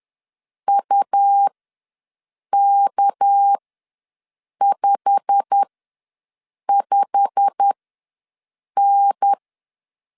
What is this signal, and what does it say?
Morse code, heard as a keyed tone, about 10 wpm, spaced as for 6 wpm, 800 Hz. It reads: UK55N
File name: LEZIONI / 8 YJ45 / 8 YJ45 ws 10wpm 3843